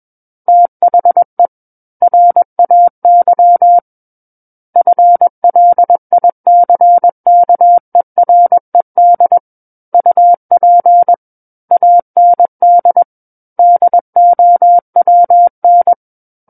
Morse code, heard as a keyed tone, about 21 wpm, 700 Hz. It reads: THE RAY FLICKERED UP AND DOWN
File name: Book One / 12-Chapter 12 - 21 WPM